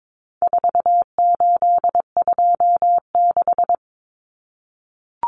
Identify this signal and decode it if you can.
Morse code, heard as a keyed tone, about 22 wpm, 700 Hz. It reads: 4826
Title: Numbers - 4-Digit 22wpm